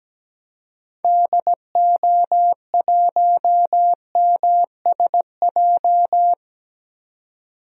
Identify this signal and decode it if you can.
Morse code, heard as a keyed tone, about 17 wpm, 700 Hz. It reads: DO1MSJ
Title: DXCC-100 Call Signs - 6 Characters - 17wpm